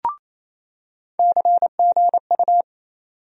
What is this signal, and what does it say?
Morse code, heard as a keyed tone, about 28 wpm, 700 Hz. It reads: CGU
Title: USA Call Sign Postfixes - 3 Letters 28wpm